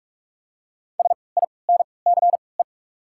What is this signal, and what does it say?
Morse code, heard as a keyed tone, about 43 wpm, spaced as for 25 wpm, 700 Hz. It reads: SINCE
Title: 3x Character Spacing - Top 500 Words - 45wpm